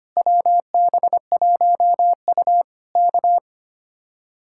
Morse code, heard as a keyed tone, about 24 wpm, 700 Hz. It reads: WB1U K
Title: Speed-Racing Common QSO Phrases Encore - 25wpm